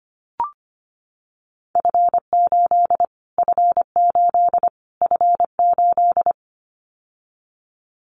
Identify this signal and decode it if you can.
Morse code, heard as a keyed tone, about 25 wpm, 700 Hz. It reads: F8 F8 F8
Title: Callsign Prefixes for Top 25 DXCC - Repeated 3x - 25wpm